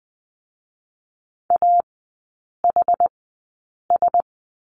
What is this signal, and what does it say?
Morse code, heard as a keyed tone, about 20 wpm, spaced as for 8 wpm, 700 Hz. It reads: AHS
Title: 100 Lesson 17e - Letter u - Three Characters 20wpm